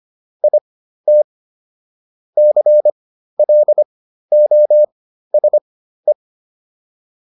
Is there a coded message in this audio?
Yes: Morse code from 0.4 to 6.1 s, heard as a keyed tone, about 25 wpm, spaced as for 13 wpm, 600 Hz. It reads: IT CLOSE